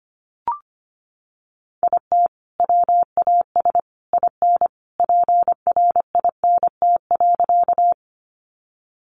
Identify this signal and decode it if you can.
Morse code, heard as a keyed tone, about 25 wpm, 700 Hz. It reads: IT WAS IN PRINT.
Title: Sentences from Top 1400 Words 25wpm